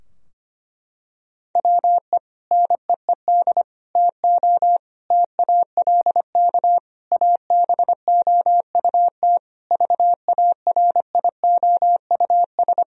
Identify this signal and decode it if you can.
Morse code, heard as a keyed tone, about 25 wpm, 700 Hz. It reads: WE NEED TO TALK ABOUT VARIOUS
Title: Sentences from Top 1000 Words - Review Part 4 - 25wpm